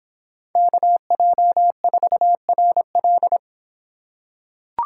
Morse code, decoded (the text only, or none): KJ4RL